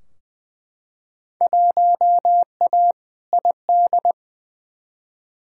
Morse code, heard as a keyed tone, about 20 wpm, 700 Hz. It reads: 1A ID
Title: Field Day Exchanges - 20wpm